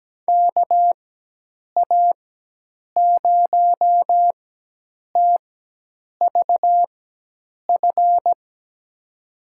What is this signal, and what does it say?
Morse code, heard as a keyed tone, about 17 wpm, spaced as for 8 wpm, 700 Hz. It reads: KA0TVF
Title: USA Call Signs - 6 Characters - 4x Character Spacing - 17wpm